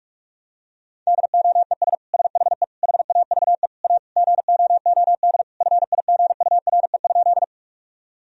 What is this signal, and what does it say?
Morse code, heard as a keyed tone, about 41 wpm, 700 Hz. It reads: DOES SHE HAVE A GOOD FIGURE?
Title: Sentences from Top 1000 Words - Review Part 2 - 45wpm